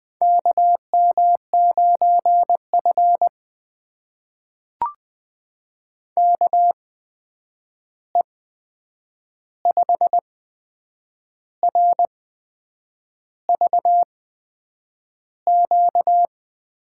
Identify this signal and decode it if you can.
Morse code, heard as a keyed tone, about 20 wpm, spaced as for 7 wpm, 700 Hz. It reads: KM9F KE5RVQ
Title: USA Call Signs - 8x Character Spacing - 20wpm